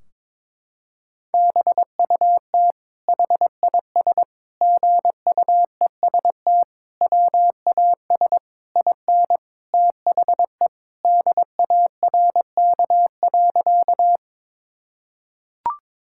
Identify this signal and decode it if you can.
Morse code, heard as a keyed tone, about 22 wpm, 700 Hz. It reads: BUT HIS GUEST WAS IN THE DARK.
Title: Sentences from Top 1600 Words 22wpm